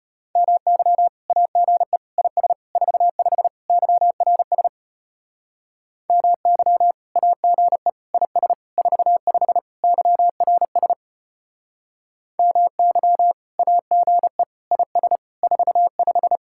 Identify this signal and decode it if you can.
Morse code, heard as a keyed tone, about 31 wpm, 700 Hz. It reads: MY AGE IS 45 YRS MY AGE IS 45 YRS MY AGE IS 45